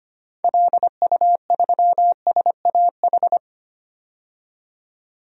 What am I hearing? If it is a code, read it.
Morse code, heard as a keyed tone, about 25 wpm, 700 Hz. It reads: LU3SAH
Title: DXCC-100 Call Signs - 6 Characters - 25wpm